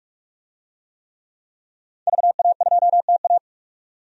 Morse code, heard as a keyed tone, about 45 wpm, 700 Hz. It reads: VA2TA
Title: DXCC-100 Call Signs - 5 Characters - 45wpm